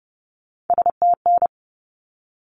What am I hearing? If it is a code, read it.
Morse code, heard as a keyed tone, about 30 wpm, 700 Hz. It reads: STN
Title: Abbreviations 30wpm